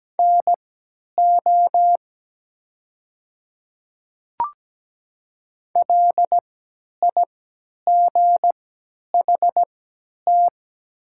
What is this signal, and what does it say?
Morse code, heard as a keyed tone, about 17 wpm, spaced as for 10 wpm, 700 Hz. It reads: NO LIGHT
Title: 3x Character Spacing - Top 200 Words - 17wpm